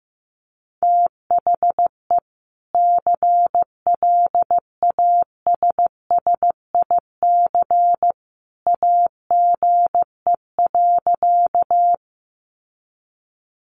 Morse code, heard as a keyed tone, about 15 wpm, 700 Hz. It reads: THE CLASSIC AGE.
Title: Sentences from Top 1000 Words 15wpm